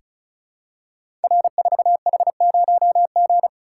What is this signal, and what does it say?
Morse code, heard as a keyed tone, about 36 wpm, 700 Hz. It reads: R4H0G
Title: Code Groups - 5 Letter:Numbers - 35wpm